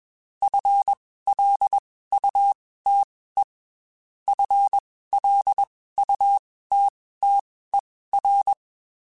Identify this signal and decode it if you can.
Morse code, heard as a keyed tone, about 21 wpm, spaced as for 15 wpm, 800 Hz. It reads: FLUTE FLUTTER